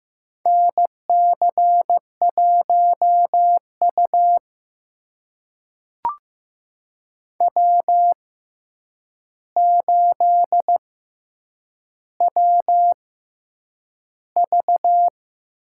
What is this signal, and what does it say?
Morse code, heard as a keyed tone, about 15 wpm, spaced as for 8 wpm, 700 Hz. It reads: NC1U W8WV
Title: USA Call Signs - 4 Characters - 6x Character Spacing - 15wpm